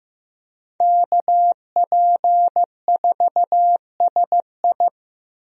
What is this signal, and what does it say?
Morse code, heard as a keyed tone, about 16 wpm, 700 Hz. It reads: KP4SI
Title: DXCC-100 Call Signs - 5 Characters - 15wpm